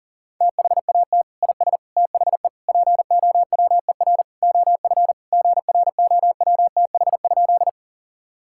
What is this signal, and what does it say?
Morse code, heard as a keyed tone, about 40 wpm, 700 Hz. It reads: THAT IS THE POWER OF GROWTH?
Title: Sentences from Top 1000 Words - Review Part 5 - 40wpm